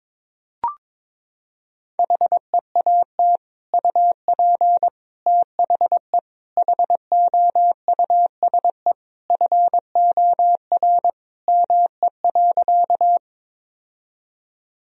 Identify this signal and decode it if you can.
Morse code, heard as a keyed tone, about 22 wpm, 700 Hz. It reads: HEAT UP THE HOUSE FOR ME.
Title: Sentences from Top 1000 Words - Review Part 3 - 22wpm